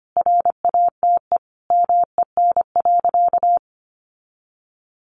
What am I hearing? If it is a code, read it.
Morse code, heard as a keyed tone, about 25 wpm, 700 Hz. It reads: RATE MEN.
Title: Sentences from Top 1300 Words 25wpm